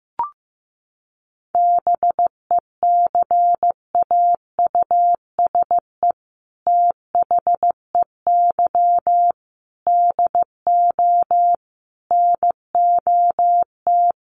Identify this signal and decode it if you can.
Morse code, heard as a keyed tone, about 15 wpm, 700 Hz. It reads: BECAUSE THEY DO NOT
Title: Sets of 5 Words Encore2 15wpm